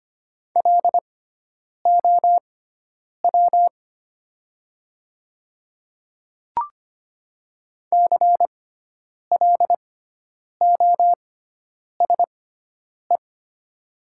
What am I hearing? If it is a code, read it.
Morse code, heard as a keyed tone, about 25 wpm, spaced as for 9 wpm, 700 Hz. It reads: LOW CLOSE
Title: 6x Character Spacing - Top 300 Words - 25wpm